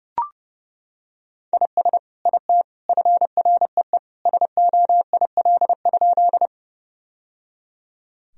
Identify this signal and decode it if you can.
Morse code, heard as a keyed tone, about 30 wpm, 700 Hz. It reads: IS IT FREE SOIL?
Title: Sentences from Top 1000 Words - Review Part 4 - 30wpm